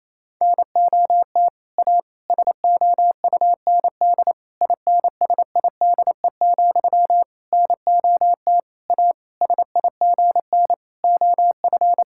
Morse code, heard as a keyed tone, about 28 wpm, 700 Hz. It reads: NOT A SOUND INSIDE, NOT A SIGN OF
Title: Sentences from Top 1000 Words - Review Part 6 - 28wpm